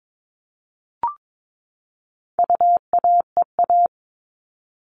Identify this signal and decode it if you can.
Morse code, heard as a keyed tone, about 22 wpm, 700 Hz. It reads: UAEA
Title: All Cut Numbers - 4-Digit 22wpm